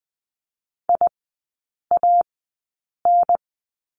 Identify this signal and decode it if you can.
Morse code, heard as a keyed tone, about 20 wpm, spaced as for 8 wpm, 700 Hz. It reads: IAN
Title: 093 Lesson 16e - Letter c - Three Characters 20wpm